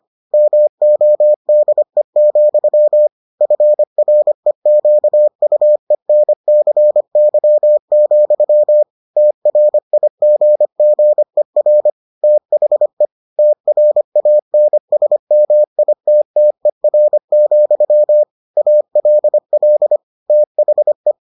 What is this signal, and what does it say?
Morse code, heard as a keyed tone, about 25 wpm, 600 Hz. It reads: MODE, FREQUENCY, TRIGGER THE TRANSMITTER, ALL THE